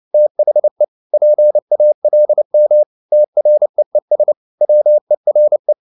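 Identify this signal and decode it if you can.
Morse code, heard as a keyed tone, about 30 wpm, 600 Hz. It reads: THE PALM TREES WERE
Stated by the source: Morse code at 29 wpm, sending Ken Follett's novel Never in English, from CW med Gnister.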